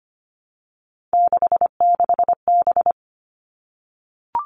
Morse code, heard as a keyed tone, about 25 wpm, 700 Hz. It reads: 66B
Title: All Cut Numbers - 3-Digit 25wpm